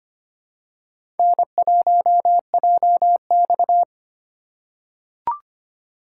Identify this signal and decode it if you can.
Morse code, heard as a keyed tone, about 25 wpm, 700 Hz. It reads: N1JX
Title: USA Call Signs - 2x Character Spacing - 25wpm